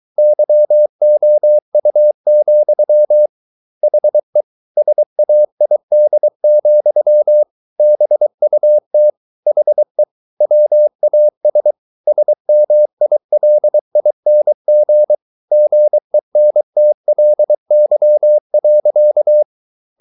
Morse code, heard as a keyed tone, about 23 wpm, 600 Hz. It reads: YOU, HE SAID, BUT HE WAS SMILING GENTLY.